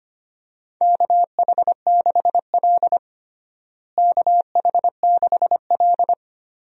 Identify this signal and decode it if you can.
Morse code, heard as a keyed tone, about 25 wpm, 700 Hz. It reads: KH6L KH6L
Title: DXCC-100 Call Signs - 4 Characters - Repeated 3x - 25wpm